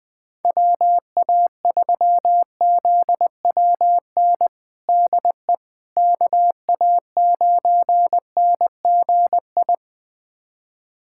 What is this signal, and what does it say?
Morse code, heard as a keyed tone, about 20 wpm, 700 Hz. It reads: WA3ZWN DE KA9NGI
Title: USA Callsign DE Callsign 20wpm